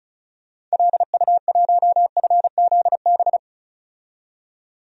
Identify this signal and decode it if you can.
Morse code, heard as a keyed tone, about 34 wpm, 700 Hz. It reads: LU1FZB
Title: DXCC-100 Call Signs - 6 Characters - 35wpm